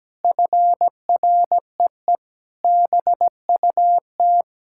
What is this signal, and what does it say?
Morse code, heard as a keyed tone, about 17 wpm, 700 Hz. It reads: FREE BUT
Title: Sentences from Top 1600 Words 17wpm